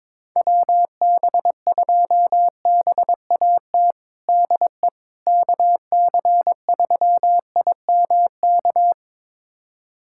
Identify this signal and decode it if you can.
Morse code, heard as a keyed tone, about 22 wpm, 700 Hz. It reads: WB2BAT DE KC3IMK
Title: USA Callsign DE Callsign - 6 Characters - 22wpm